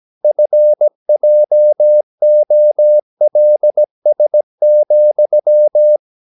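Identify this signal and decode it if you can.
Morse code, heard as a keyed tone, about 17 wpm, 600 Hz. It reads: FJOLS,
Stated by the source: Slow (QRS) Morse practice at 17 wpm, from CW med Gnister.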